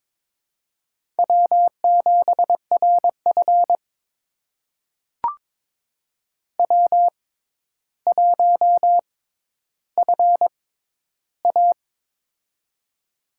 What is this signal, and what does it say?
Morse code, heard as a keyed tone, about 22 wpm, spaced as for 11 wpm, 700 Hz. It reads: W7RF W1FA
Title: USA Call Signs - 4 Characters - 6x Character Spacing - 22wpm